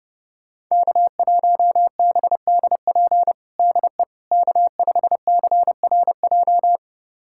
Morse code, heard as a keyed tone, about 32 wpm, 700 Hz. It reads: K1BDP DE K5CRJ
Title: USA Callsign DE Callsign - 5 Characters - 30wpm